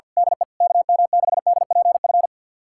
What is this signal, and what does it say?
Morse code, heard as a keyed tone, about 46 wpm, 700 Hz. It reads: DE KM6DPF
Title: USA Callsign DE Callsign - 6 Characters - 50wpm